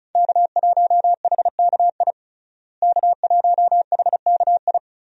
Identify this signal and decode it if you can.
Morse code, heard as a keyed tone, about 34 wpm, 700 Hz. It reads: K1HKI K1HKI
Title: USA Call Signs - 5 Characters - Repeated 2x - 35wpm